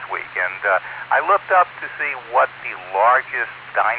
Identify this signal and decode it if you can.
Emergency Alert System (SAME) tones, heard as harsh buzzing bursts, with speech. It is heard from 0.2 to 3.7 s.